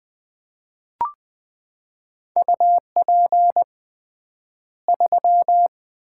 Morse code, heard as a keyed tone, about 20 wpm, 700 Hz. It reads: UP 3
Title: Common QSO Phrases - 3x Word-Spacing - 20wpm